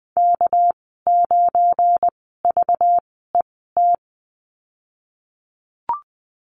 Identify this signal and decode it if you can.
Morse code, heard as a keyed tone, about 20 wpm, spaced as for 15 wpm, 700 Hz. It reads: K9VET
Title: 213 Lesson 34g - Number 8 - Call Signs 20wpm